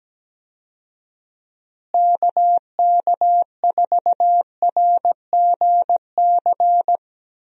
Morse code, heard as a keyed tone, about 17 wpm, 700 Hz. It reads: KK4RGC
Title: USA Call Signs - 6 Characters - Repeated 3x - 17wpm